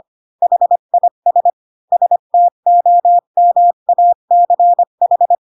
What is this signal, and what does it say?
Morse code, heard as a keyed tone, about 25 wpm, 700 Hz. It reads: HIS STOMACH
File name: Pinocchio / Pinocchio ch5 20 25 ik1hge